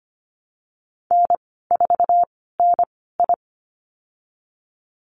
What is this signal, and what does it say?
Morse code, heard as a keyed tone, about 25 wpm, spaced as for 16 wpm, 700 Hz. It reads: N4NI
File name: Lesson 15g - Number 5 - Call Signs 25wpm